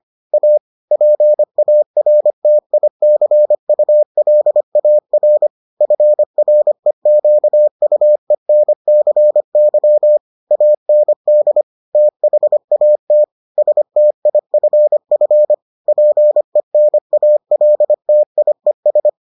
Morse code, heard as a keyed tone, about 25 wpm, 600 Hz. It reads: A PARTICULAR FREQUENCY AND THAT STIFF PENALTIES